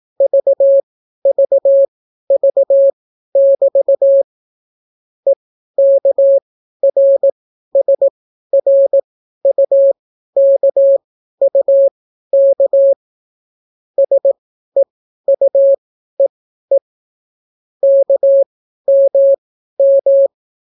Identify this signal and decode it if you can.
Morse code, heard as a keyed tone, about 18 wpm, spaced as for 12 wpm, 550 Hz. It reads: VVV<BT> EKRSRUKUK SEUEE KMM